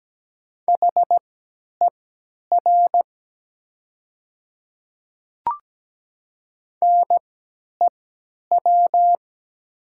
Morse code, heard as a keyed tone, about 17 wpm, spaced as for 10 wpm, 700 Hz. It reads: HER NEW
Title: 3x Character Spacing - Top 200 Words - 17wpm